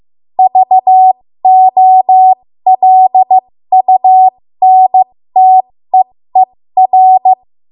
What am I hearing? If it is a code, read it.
Morse code, heard as a keyed tone, about 15 wpm, spaced as for 13 wpm, 750 Hz. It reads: VOLUNTEER
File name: Morse / 13WPM